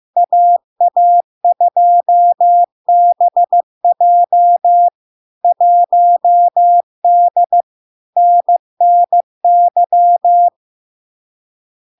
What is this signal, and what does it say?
Morse code, heard as a keyed tone, about 15 wpm, 700 Hz. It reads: AA2BJ 1D NNY
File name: LARK-FD-2022-Log-in-CW-15wpm-700-Hz